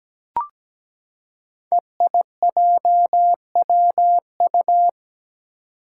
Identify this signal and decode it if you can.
Morse code, heard as a keyed tone, about 17 wpm, 700 Hz. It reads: EIJWU
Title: Five Challenging Letter-Number Combinations 17wpm